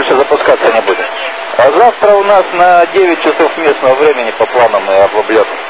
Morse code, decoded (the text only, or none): none